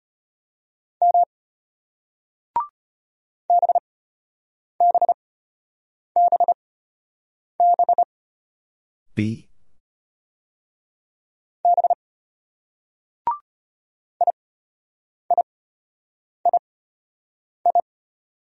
Morse code, heard as a keyed tone, about 32 wpm, spaced as for 8 wpm, 700 Hz. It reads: M BBBB B IIII